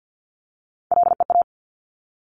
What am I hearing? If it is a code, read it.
Morse code, heard as a keyed tone, about 50 wpm, 700 Hz. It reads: LEA
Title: Top Three Letter Combinations in Words 50wpm